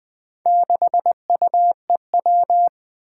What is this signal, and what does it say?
Morse code, heard as a keyed tone, about 20 wpm, 700 Hz. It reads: 6UEW
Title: Four Challenging Letter-Number Combinations 20wpm